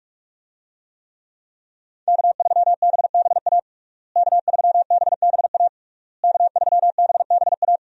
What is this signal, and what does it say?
Morse code, heard as a keyed tone, about 46 wpm, 700 Hz. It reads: K3BBA K3BBA K3BBA
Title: USA Call Signs - 5 Characters - Repeated 3x - 45wpm